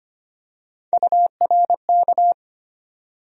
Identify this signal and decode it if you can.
Morse code, heard as a keyed tone, about 25 wpm, 700 Hz. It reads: URK
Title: USA Call Sign Postfixes - 3 Letters 25wpm